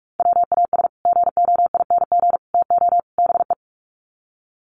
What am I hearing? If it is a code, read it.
Morse code, heard as a keyed tone, about 44 wpm, 700 Hz. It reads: WAS GOING TO BE
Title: Sets of 4 Words 45wpm